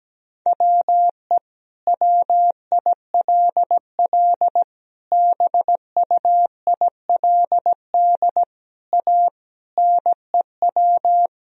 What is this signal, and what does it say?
Morse code, heard as a keyed tone, about 17 wpm, 700 Hz. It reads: WE WILL BUILD A NEW
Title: Sentences from Top 1000 Words - Review Part 3 - 17wpm